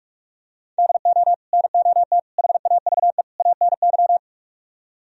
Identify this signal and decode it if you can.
Morse code, heard as a keyed tone, about 44 wpm, 700 Hz. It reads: DO NOT HAVE ANY